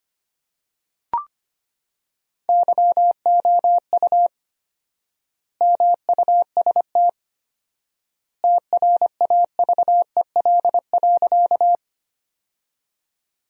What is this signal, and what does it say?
Morse code, heard as a keyed tone, about 25 wpm, 700 Hz. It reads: YOU MUST TRAVEL.
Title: 4x Word-Spacing - Sentences from Top 400 Words - 25wpm